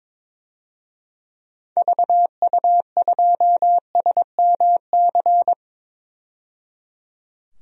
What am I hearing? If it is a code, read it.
Morse code, heard as a keyed tone, about 22 wpm, 700 Hz. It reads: VU2SMC
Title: DXCC-100 Call Signs - 6 Characters - 22wpm